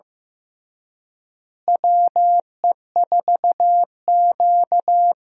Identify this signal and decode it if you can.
Morse code, heard as a keyed tone, about 15 wpm, 700 Hz. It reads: WE4Q
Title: USA Call Signs - 4 Characters - Repeated 3x - 15wpm